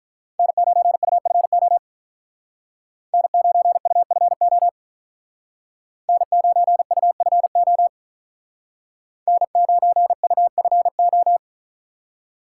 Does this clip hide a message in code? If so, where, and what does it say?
Yes: Morse code from 0.4 to 11.4 s, heard as a keyed tone, about 44 wpm, 700 Hz. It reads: N9UFO N9UFO N9UFO N9UFO